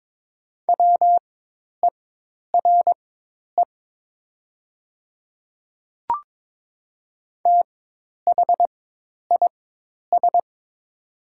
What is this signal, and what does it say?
Morse code, heard as a keyed tone, about 22 wpm, spaced as for 10 wpm, 700 Hz. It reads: WERE THIS
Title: 4x Character Spacing - Top 100 Words - 22wpm